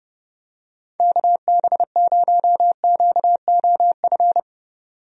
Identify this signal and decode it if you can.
Morse code, heard as a keyed tone, about 31 wpm, 700 Hz. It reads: KB0QOF
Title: USA Call Signs - 6 Characters - 30wpm